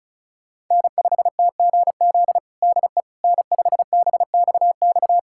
Morse code, heard as a keyed tone, about 36 wpm, 700 Hz. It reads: N5TGZ DE N5BXX
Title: USA Callsign DE Callsign - 5 Characters - 35wpm